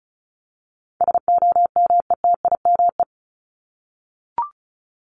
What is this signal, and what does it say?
Morse code, heard as a keyed tone, about 34 wpm, 700 Hz. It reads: SOMETIME
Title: ES:S Suffix - Top 100 Words with Base Words 35wpm